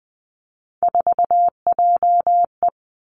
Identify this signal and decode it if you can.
Morse code, heard as a keyed tone, about 20 wpm, 700 Hz. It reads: 4JE
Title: Three Challenging Letter-Number Combinations 20wpm